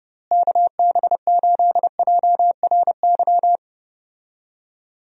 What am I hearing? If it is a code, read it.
Morse code, heard as a keyed tone, about 30 wpm, 700 Hz. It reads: KB8JRY
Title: USA Call Signs - 6 Characters - Repeated 2x - 30wpm